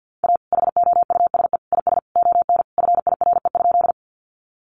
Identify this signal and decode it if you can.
Morse code, heard as a keyed tone, about 50 wpm, 700 Hz. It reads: A HOUSE IS ON FIRE?
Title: Sentences from Top 400 Words 50wpm